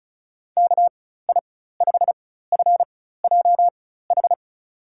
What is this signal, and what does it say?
Morse code, heard as a keyed tone, about 34 wpm, spaced as for 16 wpm, 700 Hz. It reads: KI5FJH
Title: USA Call Signs - 6 Characters - 4x Character Spacing - 35wpm